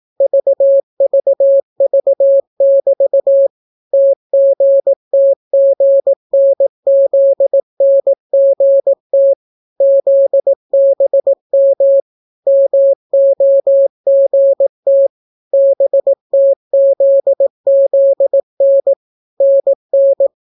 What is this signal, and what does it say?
Morse code, heard as a keyed tone, about 18 wpm, 550 Hz. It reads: VVV<BT> TGTGNZNGT ZBM MOGT BTZZN NN